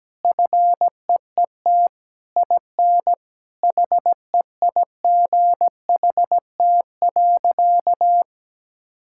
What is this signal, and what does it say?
Morse code, heard as a keyed tone, about 17 wpm, 700 Hz. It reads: FEET IN HEIGHT.
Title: Sentences from Top 1600 Words 17wpm